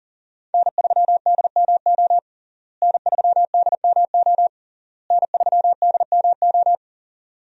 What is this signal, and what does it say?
Morse code, heard as a keyed tone, about 37 wpm, 700 Hz. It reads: N3DMO N3DMO N3DMO
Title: USA Call Signs - 5 Characters - Repeated 3x - 40wpm